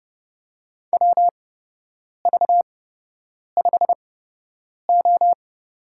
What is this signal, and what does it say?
Morse code, heard as a keyed tone, about 29 wpm, spaced as for 8 wpm, 700 Hz. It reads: WV5O
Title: USA Call Signs - 8x Character Spacing - 30wpm